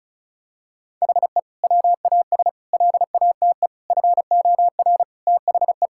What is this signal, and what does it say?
Morse code, heard as a keyed tone, about 34 wpm, 700 Hz. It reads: HE WAS LATE FOR THE
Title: Sentences from Top 1600 Words 35wpm